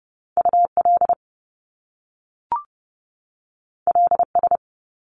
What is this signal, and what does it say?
Morse code, heard as a keyed tone, about 30 wpm, 700 Hz. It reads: UL LS
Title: Top Two Letter Combinations in Words 30wpm